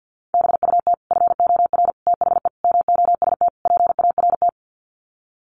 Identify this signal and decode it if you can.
Morse code, heard as a keyed tone, about 50 wpm, 700 Hz. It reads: BUT FOR THE MOST PART
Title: Sets of 5 Words Encore 50wpm